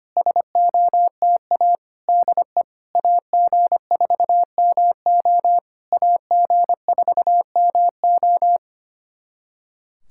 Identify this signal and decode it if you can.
Morse code, heard as a keyed tone, about 25 wpm, 690 Hz. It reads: SOTA DE AG4MO AG4MO
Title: SOTA QSO Practice 25wpm